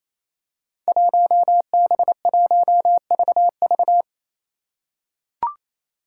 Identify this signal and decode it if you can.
Morse code, heard as a keyed tone, about 28 wpm, 700 Hz. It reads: 1B1VV
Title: Five Challenging Letter-Number Combinations 28wpm